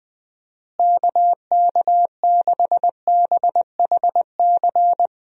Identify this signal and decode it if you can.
Morse code, heard as a keyed tone, about 20 wpm, 700 Hz. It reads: KK6BHC
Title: USA Call Signs - 6 Characters - Repeated 2x - 20wpm